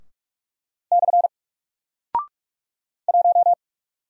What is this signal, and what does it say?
Morse code, heard as a keyed tone, about 44 wpm, 700 Hz. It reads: / 1
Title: Single Character (Letter:Number:Punctuation:Prosign) 45wpm